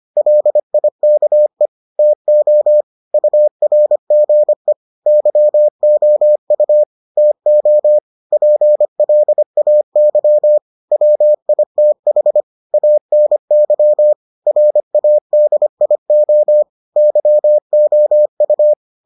Morse code, heard as a keyed tone, about 25 wpm, 600 Hz. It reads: LIKE TO URGE YOU TO PLAY WITH ANY RADIO YOU